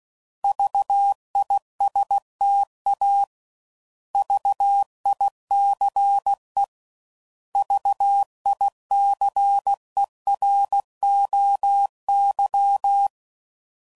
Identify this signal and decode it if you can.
Morse code, heard as a keyed tone, about 16 wpm, 800 Hz. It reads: VISTA VICE VICEROY
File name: Prefixes / By First Letter / Vs